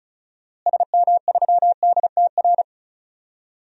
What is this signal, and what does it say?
Morse code, heard as a keyed tone, about 34 wpm, 700 Hz. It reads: SM3DTR
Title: DXCC-100 Call Signs - 6 Characters - Repeated 3x - 35wpm